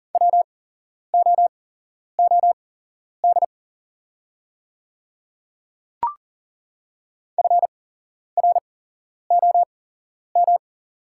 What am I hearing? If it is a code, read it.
Morse code, heard as a keyed tone, about 40 wpm, spaced as for 11 wpm, 700 Hz. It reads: WOOD FROM
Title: 8x Character Spacing - Top 500 Words - 40wpm